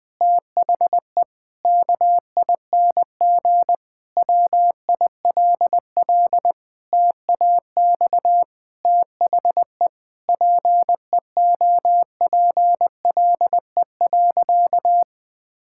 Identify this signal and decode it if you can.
Morse code, heard as a keyed tone, about 20 wpm, 700 Hz. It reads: THE KING WILL TAX THE PEOPLE.
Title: Sentences from Top 1000 Words - Review Part 6 - 20wpm